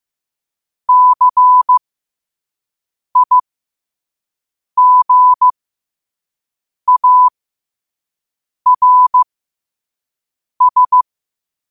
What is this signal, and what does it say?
Morse code, heard as a keyed tone, about 15 wpm, spaced as for 5 wpm, 1000 Hz. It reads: CIGARS
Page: New quotes every day in morse code at 5 Words per minute.